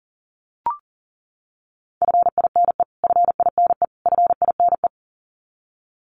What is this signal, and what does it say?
Morse code, heard as a keyed tone, about 40 wpm, 700 Hz. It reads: FINE FINE FINE
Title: Top 500 Words Repeated 40wpm